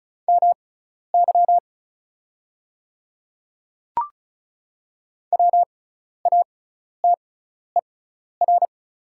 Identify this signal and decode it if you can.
Morse code, heard as a keyed tone, about 36 wpm, spaced as for 12 wpm, 700 Hz. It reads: MY WATER
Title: 6x Character Spacing - Top 100 Words - 35wpm